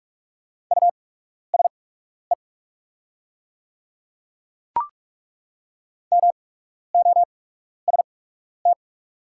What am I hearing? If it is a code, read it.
Morse code, heard as a keyed tone, about 43 wpm, spaced as for 12 wpm, 690 Hz. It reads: USE MOST
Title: 8x Character Spacing - Top 500 Words - 45wpm